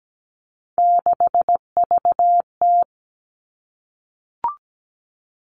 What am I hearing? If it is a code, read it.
Morse code, heard as a keyed tone, about 17 wpm, 700 Hz. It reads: 6VT
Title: Three Challenging Letter-Number Combinations 17wpm